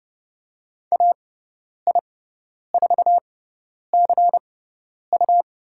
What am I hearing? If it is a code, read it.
Morse code, heard as a keyed tone, about 30 wpm, spaced as for 10 wpm, 690 Hz. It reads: AI4CU
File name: Lesson 17g - Letter u - Call Signs 30wpm